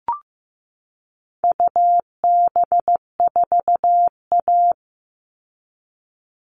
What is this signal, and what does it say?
Morse code, heard as a keyed tone, about 15 wpm, 700 Hz. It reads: UB4A
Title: All Cut Numbers - 4-Digit 15wpm